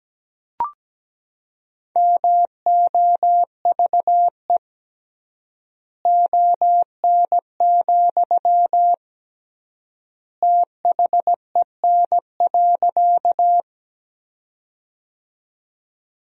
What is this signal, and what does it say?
Morse code, heard as a keyed tone, about 17 wpm, 700 Hz. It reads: MOVE ON, THEN.
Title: 3x Word-Spacing - Sentences from Top 200 Words - 17wpm